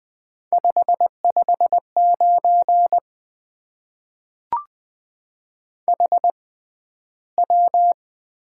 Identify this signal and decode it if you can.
Morse code, heard as a keyed tone, about 20 wpm, spaced as for 12 wpm, 700 Hz. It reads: 559 HW
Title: Top 90 QSO Elements - 6x Character Spacing - 20wpm